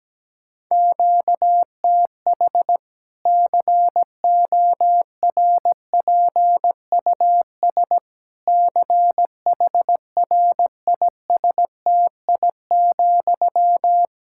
Morse code, heard as a keyed tone, about 17 wpm, 700 Hz. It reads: QTH CORPUS CHRISTI,
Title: Practice QSO - Exchange 1 - Beginning - 17wpm